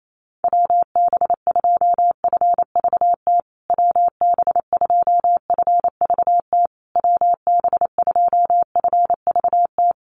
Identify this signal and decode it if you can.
Morse code, heard as a keyed tone, about 28 wpm, 700 Hz. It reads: WB2FVT WB2FVT WB2FVT
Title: USA Call Signs Repeated 3x 28wpm